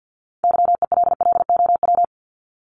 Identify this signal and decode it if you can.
Morse code, heard as a keyed tone, about 50 wpm, 700 Hz. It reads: YELLOW
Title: Top 1751-2000 Words 50wpm